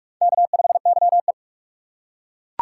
Morse code, heard as a keyed tone, about 39 wpm, 700 Hz. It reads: K5YE